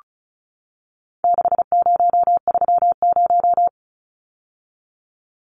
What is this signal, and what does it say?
Morse code, heard as a keyed tone, about 34 wpm, 700 Hz. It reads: 6030
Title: Numbers - 4-Digit 35wpm